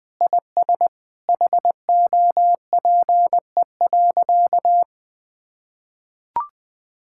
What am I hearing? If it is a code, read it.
Morse code, heard as a keyed tone, about 20 wpm, 700 Hz. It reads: IS HOPE.
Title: Sentences with 3-Word Phrases - Spoken Beginning - 20wpm